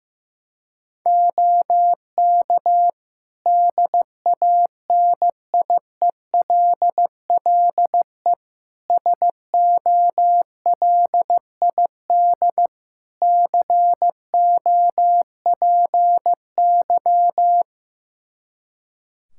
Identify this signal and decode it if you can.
Morse code, heard as a keyed tone, about 16 wpm, 700 Hz. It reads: OK DANIELLE SOLID COPY
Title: Practice QSO - Exchange 3 - More Info - 15wpm